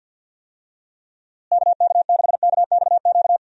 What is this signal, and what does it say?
Morse code, heard as a keyed tone, about 51 wpm, 700 Hz. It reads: KK6KXQ
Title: USA Call Signs - 6 Characters - Repeated 2x - 50wpm